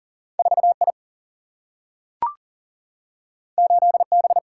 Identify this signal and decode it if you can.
Morse code, heard as a keyed tone, about 40 wpm, 700 Hz. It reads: 4I 8B
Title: Single-Letter Single-Number 40wpm